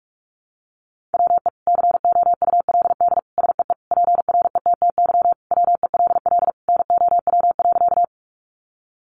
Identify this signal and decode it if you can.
Morse code, heard as a keyed tone, about 44 wpm, 700 Hz. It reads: WE COULD SEE PRETTY WELL NOW.
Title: Sentences from Top 1000 Words 45wpm